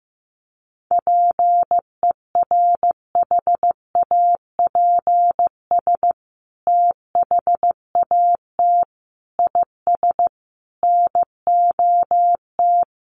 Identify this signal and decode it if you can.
Morse code, heard as a keyed tone, about 14 wpm, 700 Hz. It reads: PERHAPS THAT IS NOT
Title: Sentences from Top 900 Words 15wpm